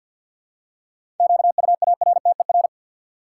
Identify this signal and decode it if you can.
Morse code, heard as a keyed tone, about 50 wpm, 700 Hz. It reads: QUARTER
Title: Seven Letter Words - 50wpm